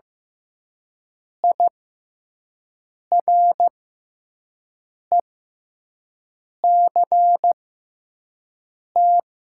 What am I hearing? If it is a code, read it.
Morse code, heard as a keyed tone, about 15 wpm, spaced as for 5 wpm, 700 Hz. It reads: IRECT
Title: 6x Character Spacing - Top 500 Words - 15wpm